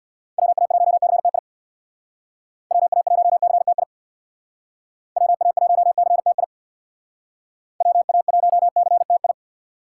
Morse code, heard as a keyed tone, about 59 wpm, 700 Hz. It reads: WA1CTI WA1CTI WA1CTI WA1CTI
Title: Speed Racing - USA Call Signs 50wpm